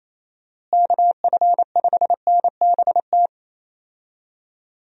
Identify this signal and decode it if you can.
Morse code, heard as a keyed tone, about 27 wpm, 700 Hz. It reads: KF5NBT
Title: USA Call Signs - 6 Characters - Repeated 3x - 28wpm